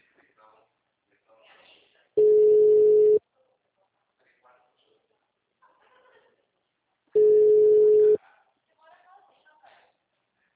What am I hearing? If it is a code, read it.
Ringback tone (Europe, 425 Hz), with speech.